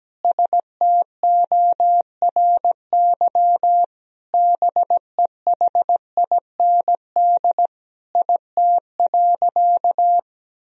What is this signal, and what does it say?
Morse code, heard as a keyed tone, about 17 wpm, 700 Hz. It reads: STORY BEHIND IT.
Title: Sentences from Top 500 Words and Abbreviations 17wpm